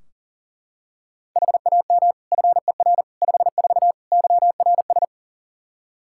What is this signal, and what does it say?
Morse code, heard as a keyed tone, about 40 wpm, 700 Hz. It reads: HAM FER 54 YRS
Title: K1USN SST QSO Practice 40wpm